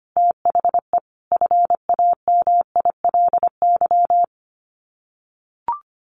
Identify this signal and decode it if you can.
Morse code, heard as a keyed tone, about 25 wpm, 700 Hz. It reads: THE FAMILY
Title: Sets of 5 Words Encore2 25wpm